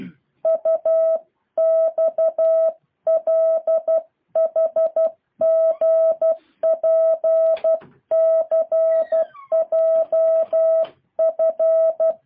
Morse code, heard as a keyed tone, about 12 wpm, 650 Hz. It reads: UXLHGPCJF
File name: jh8cbh / 2023cw